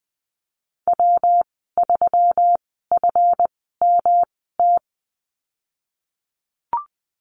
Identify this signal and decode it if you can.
Morse code, heard as a keyed tone, about 20 wpm, spaced as for 15 wpm, 700 Hz. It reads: W3FMT
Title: 147 Lesson 24g - Letter y - Call Signs 20wpm